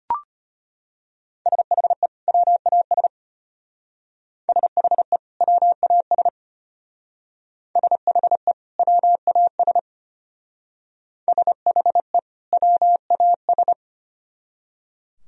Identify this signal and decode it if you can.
Morse code, heard as a keyed tone, about 33 wpm, 700 Hz. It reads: SHE WAS SHE WAS SHE WAS SHE WAS